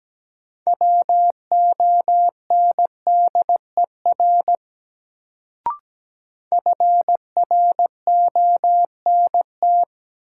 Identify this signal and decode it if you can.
Morse code, heard as a keyed tone, about 17 wpm, 700 Hz. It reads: WONDER FRONT
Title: Sentences from Top 500 Words - New Word Drill - 17wpm